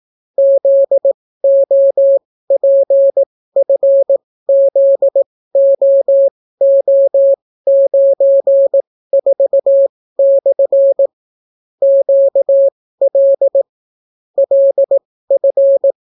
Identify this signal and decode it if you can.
Morse code, heard as a keyed tone, about 18 wpm, spaced as for 14 wpm, 550 Hz. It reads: ZOPFZOO94/ QL LF